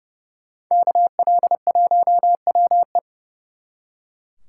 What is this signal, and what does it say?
Morse code, heard as a keyed tone, about 38 wpm, 700 Hz. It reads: KL1WE